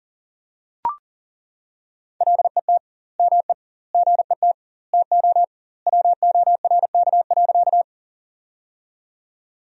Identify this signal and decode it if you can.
Morse code, heard as a keyed tone, about 40 wpm, 700 Hz. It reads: LET ME GET TO WORK.
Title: 2x Word-Spacing - Sentences from Top 300 Words - 40wpm